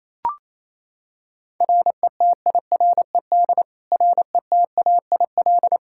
Morse code, heard as a keyed tone, about 28 wpm, 700 Hz. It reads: RETIRED RETAIL
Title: Practice QSO - Exchange 3 - More Info - 28wpm